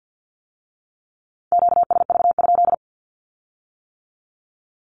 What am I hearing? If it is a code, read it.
Morse code, heard as a keyed tone, about 50 wpm, 700 Hz. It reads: QSV?
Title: Q-Codes with Spoken Meaning - 50wpm